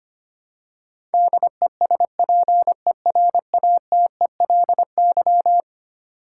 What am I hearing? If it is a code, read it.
Morse code, heard as a keyed tone, about 25 wpm, 700 Hz. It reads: DESPERATELY
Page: morse-code-ninja-practice-25wpm